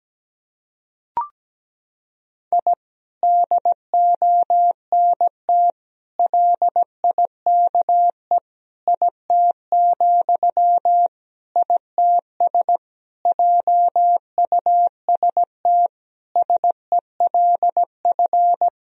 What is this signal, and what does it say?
Morse code, heard as a keyed tone, about 17 wpm, 700 Hz. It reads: I DONT LIKE IT, ITS JUST SELF
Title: Sentences from Top 1000 Words - Review Part 1 - 17wpm